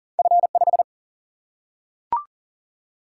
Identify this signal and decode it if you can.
Morse code, heard as a keyed tone, about 40 wpm, 700 Hz. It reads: F5
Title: Single-Letter Single-Number 40wpm